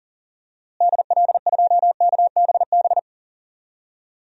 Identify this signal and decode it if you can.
Morse code, heard as a keyed tone, about 37 wpm, 700 Hz. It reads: DL2KBB
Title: DXCC-100 Call Signs - 6 Characters - Repeated 3x - 40wpm